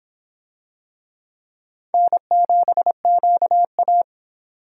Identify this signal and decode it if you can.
Morse code, heard as a keyed tone, about 25 wpm, 700 Hz. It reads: N7QA